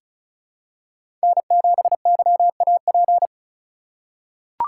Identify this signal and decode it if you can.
Morse code, heard as a keyed tone, about 33 wpm, 690 Hz. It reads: N7YAP
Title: USA Call Signs - 5 Characters - 3x Character Spacing - 35wpm